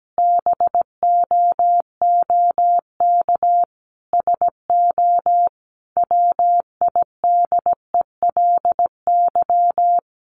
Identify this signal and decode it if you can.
Morse code, heard as a keyed tone, about 17 wpm, 700 Hz. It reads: BOOK SO WIDELY
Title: Sentences from Top 1200 Words 17wpm